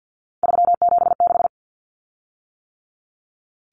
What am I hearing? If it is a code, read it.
Morse code, heard as a keyed tone, about 50 wpm, 700 Hz. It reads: 376
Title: Common Cut Numbers - 3-Digit 50wpm